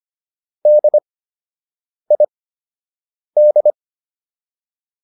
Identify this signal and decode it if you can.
Morse code, heard as a keyed tone, about 25 wpm, spaced as for 7 wpm, 600 Hz. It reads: DID